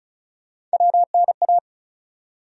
Morse code, heard as a keyed tone, about 36 wpm, 700 Hz. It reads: WNA
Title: USA Call Sign Postfixes - 3 Letters 35wpm